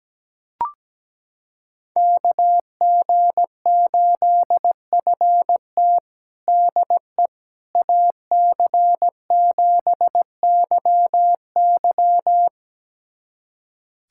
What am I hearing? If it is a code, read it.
Morse code, heard as a keyed tone, about 17 wpm, 700 Hz. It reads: KG8FT DE AC7YY
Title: USA Callsign DE Callsign - 5 Characters - 17wpm